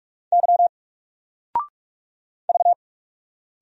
Morse code, heard as a keyed tone, about 44 wpm, 700 Hz. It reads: Y V
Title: Single Letter-Number Alt 45wpm